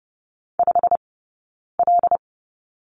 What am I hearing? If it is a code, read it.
Morse code, heard as a keyed tone, about 30 wpm, spaced as for 9 wpm, 690 Hz. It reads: 5L
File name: Lesson 17d - Letter u - Two Characters 30wpm